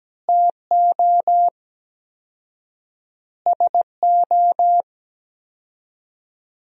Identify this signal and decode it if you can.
Morse code, heard as a keyed tone, about 17 wpm, 700 Hz. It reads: TO SO
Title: 4x Word-Spacing - Sentences from Top 400 Words - 17wpm